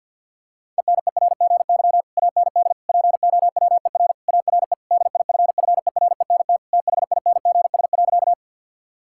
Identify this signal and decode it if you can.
Morse code, heard as a keyed tone, about 50 wpm, 700 Hz. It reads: ENERGY AND POWER ARE DIFFERENT THINGS.
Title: Sentences from Top 1000 Words - Review Part 6 - 50wpm